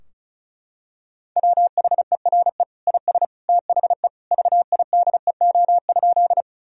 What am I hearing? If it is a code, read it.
Morse code, heard as a keyed tone, about 36 wpm, 700 Hz. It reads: WHERE IS THE VIDEO?
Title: Sentences from Top 1000 Words - Review Part 4 - 35wpm